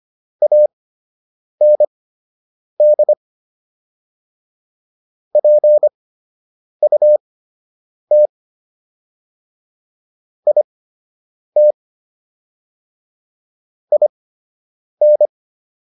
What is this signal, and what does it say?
Morse code, heard as a keyed tone, about 25 wpm, spaced as for 8 wpm, 600 Hz. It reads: AND PUT IT IN